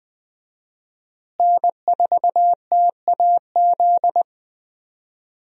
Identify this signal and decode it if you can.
Morse code, heard as a keyed tone, about 20 wpm, 700 Hz. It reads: N4TAZ
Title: USA Call Signs - 5 Characters - Repeated 2x - 20wpm